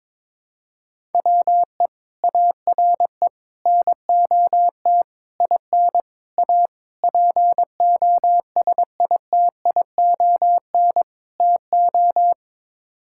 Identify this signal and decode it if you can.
Morse code, heard as a keyed tone, about 22 wpm, 700 Hz. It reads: WE ARE NOT IN A POSITION TO
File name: practice / 22wpm / Sets of 7 Words - 22wpm